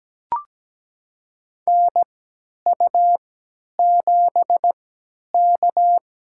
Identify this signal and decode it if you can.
Morse code, heard as a keyed tone, about 17 wpm, spaced as for 10 wpm, 700 Hz. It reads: NU7K
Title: USA Call Signs - 3x Character Spacing - 17wpm